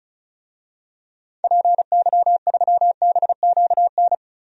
Morse code, heard as a keyed tone, about 34 wpm, 700 Hz. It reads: PY3BQN
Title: DXCC-100 Call Signs - 6 Characters - Repeated 2x - 35wpm